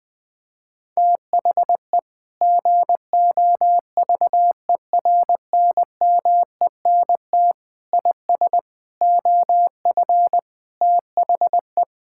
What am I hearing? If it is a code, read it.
Morse code, heard as a keyed tone, about 20 wpm, 700 Hz. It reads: THE GOVERNMENT IS OF THE
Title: Sentences with 9-Word Phrases - Spoken Beginning - 20wpm